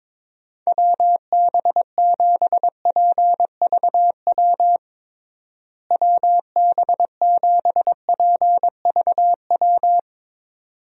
Morse code, heard as a keyed tone, about 22 wpm, 700 Hz. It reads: WB7PVW WB7PVW
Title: USA Call Signs - 6 Characters - Repeated 2x - 22wpm